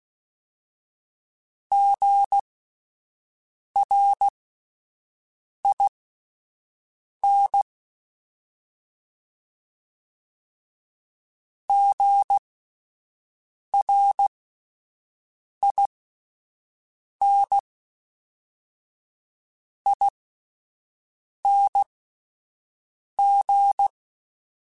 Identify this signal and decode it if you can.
Morse code, heard as a keyed tone, about 16 wpm, spaced as for 5 wpm, 800 Hz. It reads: GRIN GRINING